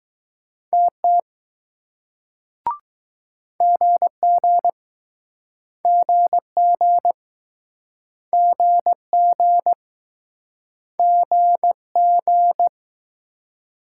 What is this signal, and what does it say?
Morse code, heard as a keyed tone, about 19 wpm, 700 Hz. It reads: TT GG GG GG G G